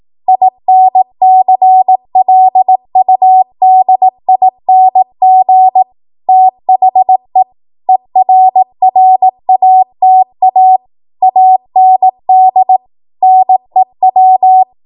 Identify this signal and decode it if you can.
Morse code, heard as a keyed tone, about 18 wpm, 750 Hz. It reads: INCLUDING THE ERRATA AND NEW